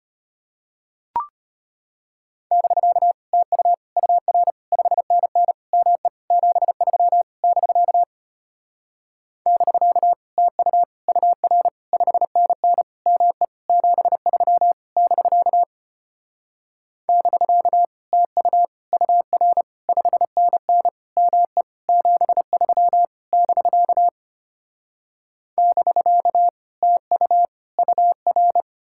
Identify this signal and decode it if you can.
Morse code, heard as a keyed tone, about 33 wpm, 700 Hz. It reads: <BK> TU UR 5NN ME 73 <BK> <BK> TU UR 5NN ME 73 <BK> <BK> TU UR 5NN ME 73 <BK> <BK> TU UR